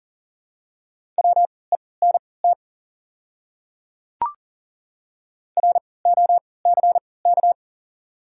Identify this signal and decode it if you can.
Morse code, heard as a keyed tone, about 40 wpm, spaced as for 23 wpm, 690 Hz. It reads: WENT ROCK
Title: 3x Character Spacing - Top 500 Words - 40wpm